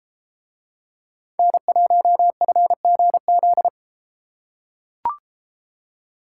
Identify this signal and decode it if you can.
Morse code, heard as a keyed tone, about 32 wpm, 700 Hz. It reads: N1FGZ